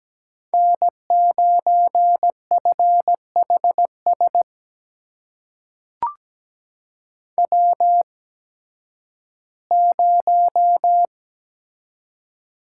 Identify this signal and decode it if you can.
Morse code, heard as a keyed tone, about 17 wpm, spaced as for 11 wpm, 700 Hz. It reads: N9FHS W0
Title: USA Call Signs - 5 Characters - 8x Character Spacing - 17wpm